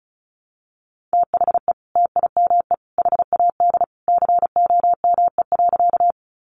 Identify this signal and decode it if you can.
Morse code, heard as a keyed tone, about 34 wpm, 700 Hz. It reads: THE TIME HAD COME.
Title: Sentences from Top 100 Words 35wpm